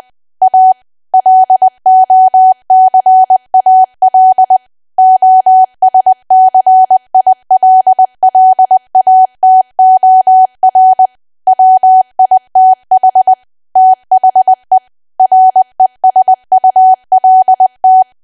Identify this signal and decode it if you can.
Morse code, heard as a keyed tone, about 20 wpm, 750 Hz. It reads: A LOCAL OSCILLATOR WITH THE RESULT
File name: cwa / W1AW-Level3 / 20WPM